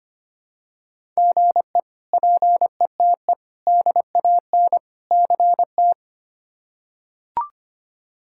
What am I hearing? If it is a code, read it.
Morse code, heard as a keyed tone, about 25 wpm, 700 Hz. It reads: GE PETE DAN CT
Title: K1USN SST QSO Practice - 25wpm